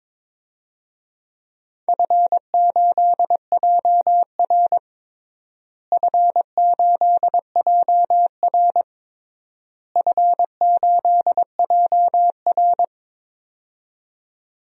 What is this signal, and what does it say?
Morse code, heard as a keyed tone, about 22 wpm, 700 Hz. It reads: F8JR F8JR F8JR
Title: DXCC-100 Call Signs - 4 Characters - Repeated 3x - 22wpm